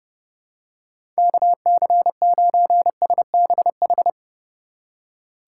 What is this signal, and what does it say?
Morse code, heard as a keyed tone, about 30 wpm, 700 Hz. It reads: KC9SBH
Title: USA Call Signs - 6 Characters - 3x Character Spacing - 30wpm